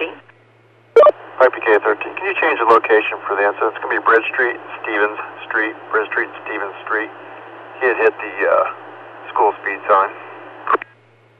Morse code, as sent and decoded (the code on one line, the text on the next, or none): none